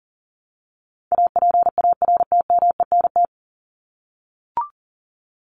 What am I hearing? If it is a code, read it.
Morse code, heard as a keyed tone, about 40 wpm, 700 Hz. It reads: APARTMENT
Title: Top 751-1000 Words 40wpm